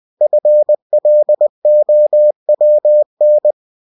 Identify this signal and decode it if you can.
Morse code, heard as a keyed tone, about 20 wpm, 600 Hz. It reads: FLOWN
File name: cw-books / winnie-the-pooh / winnie-20wpm-page-0100